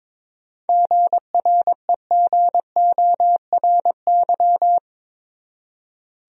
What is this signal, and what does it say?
Morse code, heard as a keyed tone, about 22 wpm, 700 Hz. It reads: GREGORY
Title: USA Common Names - 22wpm